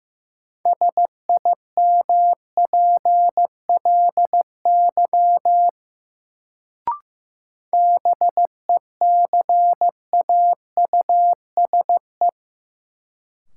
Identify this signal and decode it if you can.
Morse code, heard as a keyed tone, about 15 wpm, 700 Hz. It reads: SIMPLY BECAUSE
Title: Sentences from Top 600 Words - New Word Drill - 15wpm